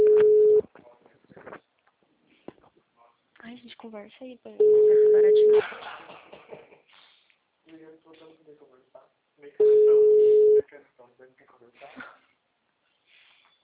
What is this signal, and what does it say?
Ringback tone (Europe, 425 Hz), with speech.